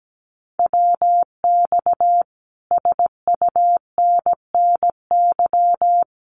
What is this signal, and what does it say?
Morse code, heard as a keyed tone, about 17 wpm, 700 Hz. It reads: WX SUNNY
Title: Common QSO Phrases Encore 17wpm